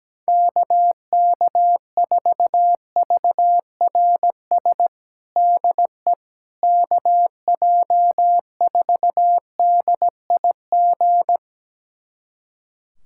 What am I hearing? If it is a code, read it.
Morse code, heard as a keyed tone, about 17 wpm, 700 Hz. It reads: KK4VRS DE KJ4DIG
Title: USA Callsign DE Callsign - 6 Characters - 17wpm